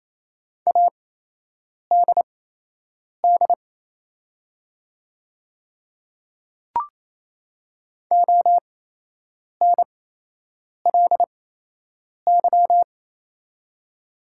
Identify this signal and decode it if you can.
Morse code, heard as a keyed tone, about 29 wpm, spaced as for 8 wpm, 700 Hz. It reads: ADD ONLY
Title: 8x Character Spacing - Top 300 Words - 28wpm